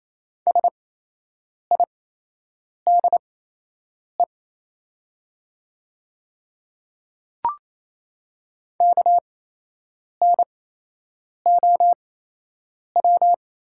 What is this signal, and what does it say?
Morse code, heard as a keyed tone, about 27 wpm, spaced as for 8 wpm, 700 Hz. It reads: SIDE KNOW
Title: 8x Character Spacing - Top 300 Words - 28wpm